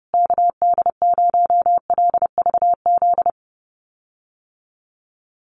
Morse code, heard as a keyed tone, about 30 wpm, 700 Hz. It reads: KD0LVZ
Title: USA Call Signs Repeated 3x 30wpm